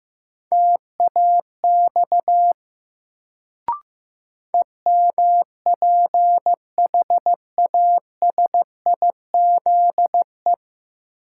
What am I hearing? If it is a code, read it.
Morse code, heard as a keyed tone, about 15 wpm, 700 Hz. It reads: TAX EMPHASIZE
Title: Sentences from Top 1500 Words - New Word Drill Alt - 15wpm